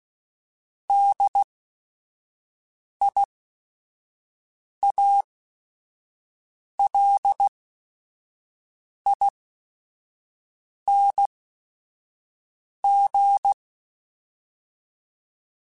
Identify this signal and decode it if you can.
Morse code, heard as a keyed tone, about 16 wpm, spaced as for 5 wpm, 800 Hz. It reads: DIALING
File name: Special INGs / Special INGs-mp3s / Sngl Ing @5